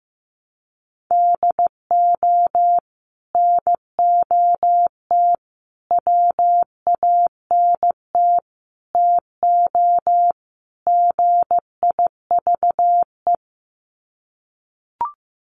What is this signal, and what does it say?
Morse code, heard as a keyed tone, about 15 wpm, 700 Hz. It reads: DO NOT WANT TO GIVE
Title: Sets of 5 Words Encore 15wpm